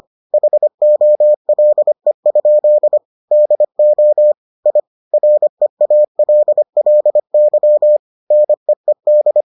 Morse code, heard as a keyed tone, about 25 wpm, 600 Hz. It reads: HOLE? DO I REALLY NEED